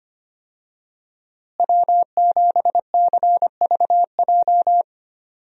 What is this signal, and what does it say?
Morse code, heard as a keyed tone, about 25 wpm, 700 Hz. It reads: W7CVJ
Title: USA Call Signs - 5 Characters - Repeated 2x - 25wpm